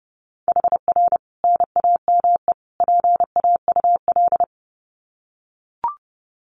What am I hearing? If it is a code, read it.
Morse code, heard as a keyed tone, about 30 wpm, 700 Hz. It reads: HR NAME PAUL
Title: Common QSO Phrases 30wpm